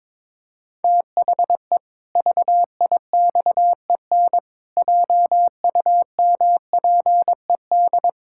Morse code, heard as a keyed tone, about 22 wpm, 700 Hz. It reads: THE VIXEN JUMPED
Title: Pangram Challenge v2 - 22wpm